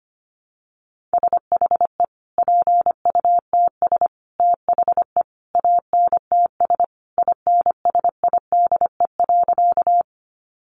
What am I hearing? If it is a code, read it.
Morse code, heard as a keyed tone, about 25 wpm, 700 Hz. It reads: SHE PUTS THE ANTS INSIDE.
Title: Additional Easy Phrases 25wpm